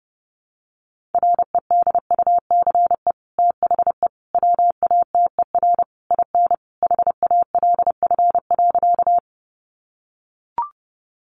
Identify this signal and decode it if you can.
Morse code, heard as a keyed tone, about 30 wpm, 700 Hz. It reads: REDUCE THE WATER IN HALF.
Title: Sentences from Top 1100 Words 30wpm